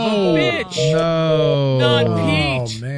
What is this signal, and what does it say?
Telephone call progress tone (North America), heard as a telephone network's tone, about 480, 620 Hz, with speech. Reorder tone.